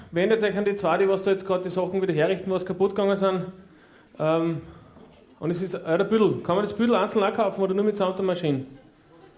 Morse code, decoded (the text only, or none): none